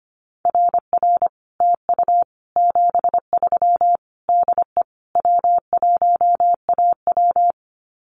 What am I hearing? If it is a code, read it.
Morse code, heard as a keyed tone, about 25 wpm, 700 Hz. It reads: RR TU 73 DE W1AW
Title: Common QSO Phrases Encore 25wpm